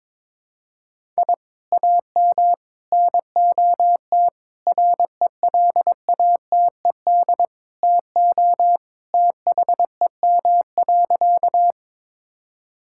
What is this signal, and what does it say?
Morse code, heard as a keyed tone, about 22 wpm, 700 Hz. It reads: I AM NOT RELATED TO THEM.
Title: Sentences from Top 1000 Words - Review Part 5 - 22wpm